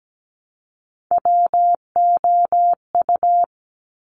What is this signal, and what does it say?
Morse code, heard as a keyed tone, about 17 wpm, 700 Hz. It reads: WOU
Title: Top Three Letter Combinations in Words 17wpm